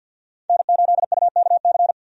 Morse code, heard as a keyed tone, about 50 wpm, 700 Hz. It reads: N8UKC
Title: USA Call Signs - 5 Characters - 50wpm